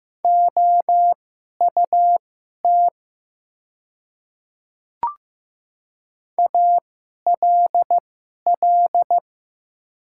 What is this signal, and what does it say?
Morse code, heard as a keyed tone, about 15 wpm, spaced as for 11 wpm, 700 Hz. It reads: OUT ALL
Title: 2x Character Spacing - Top 100 Words - 15wpm